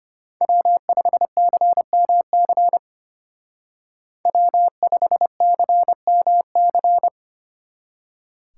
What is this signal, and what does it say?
Morse code, heard as a keyed tone, about 30 wpm, 700 Hz. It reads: W5CMC W5CMC
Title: Speed Racing - USA Call Signs 25wpm